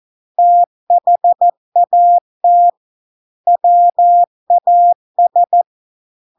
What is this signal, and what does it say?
Morse code, heard as a keyed tone, about 14 wpm, 700 Hz. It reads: THAT WAS